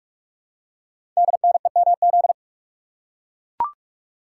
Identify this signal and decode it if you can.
Morse code, heard as a keyed tone, about 44 wpm, 700 Hz. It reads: DNEMZ
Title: Code Groups - 5 Letters - 45wpm